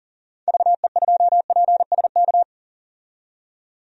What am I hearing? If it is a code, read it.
Morse code, heard as a keyed tone, about 40 wpm, 700 Hz. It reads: VE2PSK
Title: DXCC-100 Call Signs - 6 Characters - 40wpm